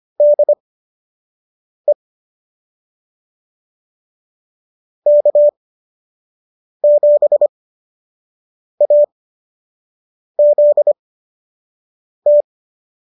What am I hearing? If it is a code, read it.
Morse code, heard as a keyed tone, about 25 wpm, spaced as for 6 wpm, 590 Hz. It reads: DE K7AZT